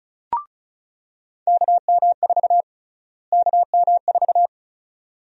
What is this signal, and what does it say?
Morse code, heard as a keyed tone, about 36 wpm, 700 Hz. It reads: KM4 KM4
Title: USA Call Sign Prefixes - Repeated 2x - 35wpm